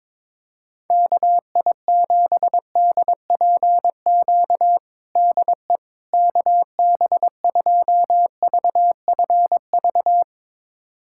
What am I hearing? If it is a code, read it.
Morse code, heard as a keyed tone, about 22 wpm, 700 Hz. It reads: KI7DPQ DE KB2VFV
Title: USA Callsign DE Callsign - 6 Characters - 22wpm